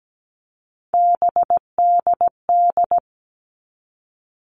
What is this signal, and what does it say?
Morse code, heard as a keyed tone, about 17 wpm, 700 Hz. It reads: BDD
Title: All Cut Numbers - 3-Digit 17wpm